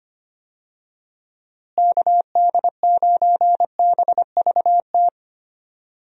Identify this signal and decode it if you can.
Morse code, heard as a keyed tone, about 25 wpm, 700 Hz. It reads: KD9BVT
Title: USA Call Signs - 6 Characters - Repeated 2x - 25wpm